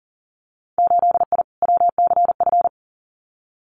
Morse code, heard as a keyed tone, about 40 wpm, 700 Hz. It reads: 8I WCF
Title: Winter Field Day Exchanges Encore 40wpm